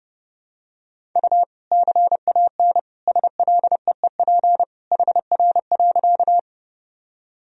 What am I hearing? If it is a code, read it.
Morse code, heard as a keyed tone, about 30 wpm, 700 Hz. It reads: U CAN SLEEP HR.
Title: Sentences from Top 500 Words and Abbreviations 30wpm